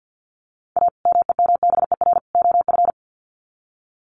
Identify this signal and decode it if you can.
Morse code, heard as a keyed tone, about 53 wpm, 700 Hz. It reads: A MEMBER OF
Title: Sets of 3 Words 50wpm